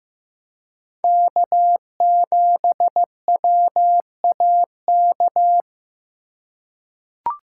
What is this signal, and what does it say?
Morse code, heard as a keyed tone, about 16 wpm, 700 Hz. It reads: K7WAK
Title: USA Call Signs - 3x Character Spacing - 15wpm